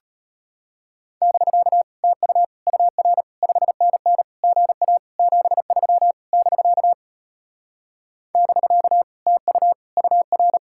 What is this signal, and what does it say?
Morse code, heard as a keyed tone, about 33 wpm, 700 Hz. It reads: <BK> TU UR 5NN GA 73 <BK> <BK> TU UR